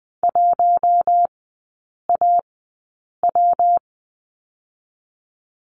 Morse code, heard as a keyed tone, about 20 wpm, 700 Hz. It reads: 1 A W
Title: 114 Lesson 19e - Letter w - Three Characters 20wpm